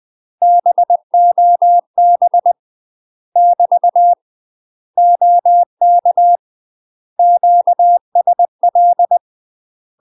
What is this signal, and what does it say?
Morse code, heard as a keyed tone, about 20 wpm, 700 Hz. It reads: BOB <BT> OK QSL
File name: UK MASTER QSOs TRAINING / 20WPM / UK MASTER QSOs 0047